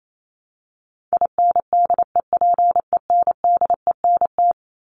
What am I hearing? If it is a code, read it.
Morse code, heard as a keyed tone, about 28 wpm, 700 Hz. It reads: INDEPENDENT
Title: Top 751-1000 Words 28wpm